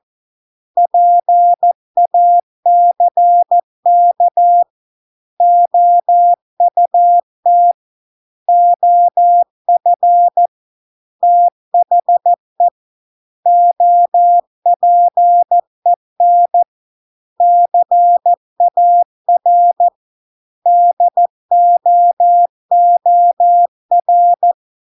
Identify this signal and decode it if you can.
Morse code, heard as a keyed tone, about 14 wpm, 700 Hz. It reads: PACK OUT OF THE OPEN CAR DOOR